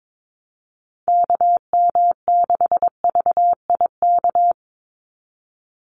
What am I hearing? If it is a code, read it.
Morse code, heard as a keyed tone, about 22 wpm, 700 Hz. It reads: KM6VIK
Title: USA Call Signs 22wpm